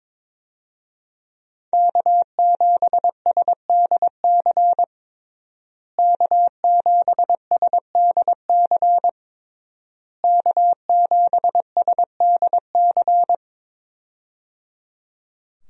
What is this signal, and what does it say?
Morse code, heard as a keyed tone, about 22 wpm, 700 Hz. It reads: K7SDC K7SDC K7SDC
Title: USA Call Signs - 5 Characters - Repeated 3x - 22wpm